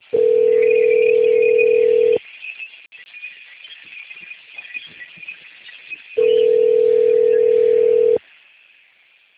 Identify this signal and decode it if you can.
Ringback tone (North America), 440 + 480 Hz.
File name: out- - - - - .12984